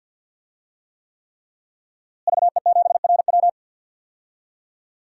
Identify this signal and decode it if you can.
Morse code, heard as a keyed tone, about 46 wpm, 700 Hz. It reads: VE7RW
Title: DXCC-100 Call Signs - 5 Characters - Repeated 3x - 50wpm